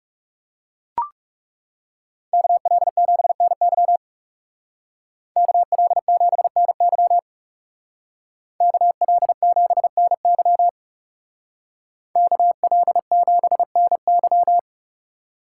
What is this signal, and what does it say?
Morse code, heard as a keyed tone, about 37 wpm, 700 Hz. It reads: KL7NY KL7NY KL7NY KL7NY
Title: Speed Racing - USA Call Signs 30wpm